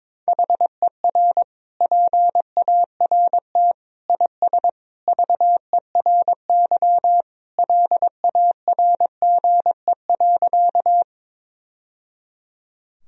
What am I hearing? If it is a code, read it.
Morse code, heard as a keyed tone, about 22 wpm, 700 Hz. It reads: HER PART IS VERY LARGE.
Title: Sentences from Top 1000 Words - Review Part 1 - 22wpm